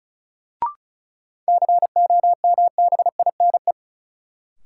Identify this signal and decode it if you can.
Morse code, heard as a keyed tone, about 36 wpm, 700 Hz. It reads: COMBINE
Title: Sentences from Top 1600 Words - New Word Drill - 35wpm